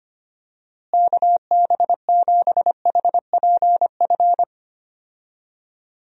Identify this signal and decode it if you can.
Morse code, heard as a keyed tone, about 25 wpm, 700 Hz. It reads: KB7HPF
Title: USA Call Signs - 6 Characters - Repeated 2x - 25wpm